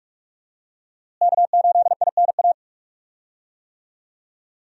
Morse code, heard as a keyed tone, about 46 wpm, 700 Hz. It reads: K8INA
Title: USA Call Signs - 5 Characters - Repeated 2x - 45wpm